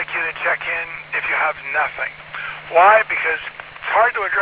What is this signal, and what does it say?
Emergency Alert System (SAME) tones, heard as harsh buzzing bursts, with speech. It is heard from 0.4 to 3.6 s.